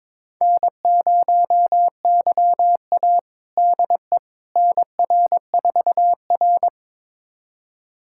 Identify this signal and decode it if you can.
Morse code, heard as a keyed tone, about 22 wpm, 700 Hz. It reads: N0YA DE NR4R
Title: USA Callsign DE Callsign - 4 Characters - 22wpm